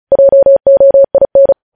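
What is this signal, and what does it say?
Morse code, heard as a keyed tone, about 36 wpm, 550 Hz. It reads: JOIN
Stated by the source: Morse code transmission of news headlines, preceded by vvv.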